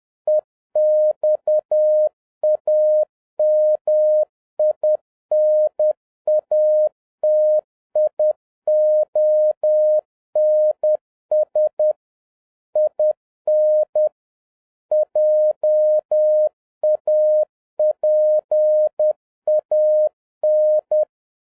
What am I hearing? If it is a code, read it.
Morse code, heard as a keyed tone, about 10 wpm, 610 Hz. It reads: EXAMINATIONS IN JAPAN